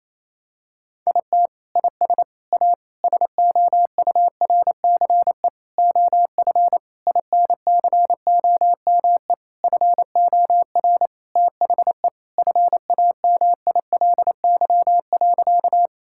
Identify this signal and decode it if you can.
Morse code, heard as a keyed tone, about 27 wpm, 700 Hz. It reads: IT IS A SOURCE OF INCOME FOR THE FAMILY.
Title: Sentences from Top 1000 Words - Review Part 5 - 28wpm